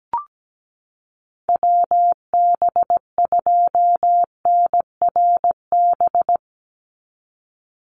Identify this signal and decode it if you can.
Morse code, heard as a keyed tone, about 17 wpm, 700 Hz. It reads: WB2NRB
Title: USA Call Signs 17wpm